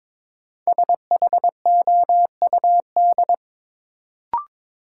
Morse code, heard as a keyed tone, about 22 wpm, 700 Hz. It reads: SHOUD
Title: Sentences from Top 1000 Words - Review Part 4 - New Word Drill - 22wpm